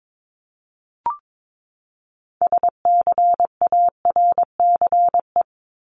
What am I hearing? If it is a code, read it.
Morse code, heard as a keyed tone, about 22 wpm, 700 Hz. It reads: SCARCE
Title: LY Suffix - Top 100 Words with Base Words 22wpm